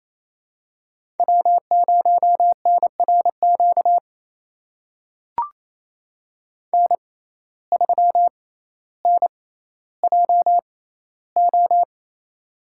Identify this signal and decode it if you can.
Morse code, heard as a keyed tone, about 28 wpm, spaced as for 14 wpm, 700 Hz. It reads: W0NRQ N3NJO
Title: USA Call Signs - 5 Characters - 6x Character Spacing - 28wpm